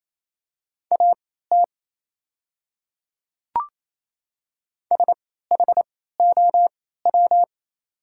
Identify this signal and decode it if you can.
Morse code, heard as a keyed tone, about 27 wpm, spaced as for 16 wpm, 690 Hz. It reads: AT SHOW
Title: 3x Character Spacing - Top 500 Words - 28wpm